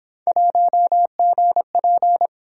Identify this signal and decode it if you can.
Morse code, heard as a keyed tone, about 28 wpm, 700 Hz. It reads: 1GP